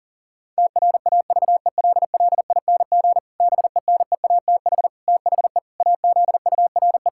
Morse code, heard as a keyed tone, about 40 wpm, 700 Hz. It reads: TRAVELLING BENEATH THE AZURE
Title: Pangram Challenge v2 - 40wpm